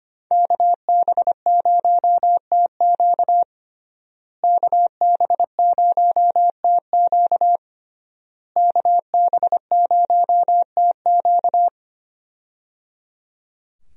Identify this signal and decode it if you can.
Morse code, heard as a keyed tone, about 25 wpm, 700 Hz. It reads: KB0TQ KB0TQ KB0TQ
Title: USA Call Signs - 5 Characters - Repeated 3x - 25wpm